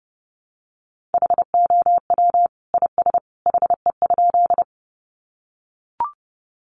Morse code, heard as a keyed tone, about 30 wpm, 700 Hz. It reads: HOW IS HE?
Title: Sentences from Top 100 Words 30wpm